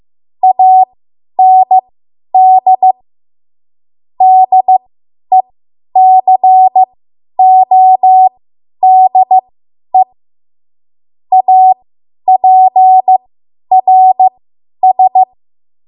Morse code, heard as a keyed tone, about 15 wpm, spaced as for 10 wpm, 750 Hz. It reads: AND DECODE APRS
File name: Morse / 10WPM